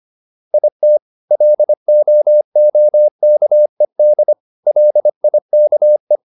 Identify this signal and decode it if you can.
Morse code, heard as a keyed tone, about 25 wpm, 600 Hz. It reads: IT LOOKED LIKE